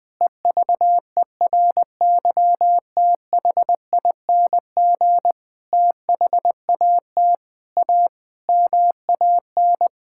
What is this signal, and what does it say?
Morse code, heard as a keyed tone, about 21 wpm, 700 Hz. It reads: EVERYTHING THAT A MAN
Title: Sentences from Top 1000 Words - Review Part 5 - 20wpm